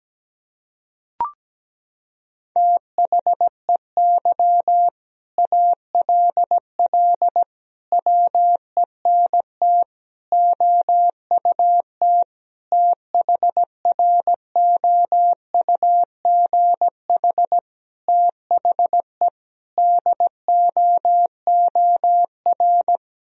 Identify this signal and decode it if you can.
Morse code, heard as a keyed tone, about 17 wpm, 700 Hz. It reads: THEY ALL WENT OUT THROUGH THE DOOR
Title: Sentences from Top 1000 Words - Review Part 1 - 17wpm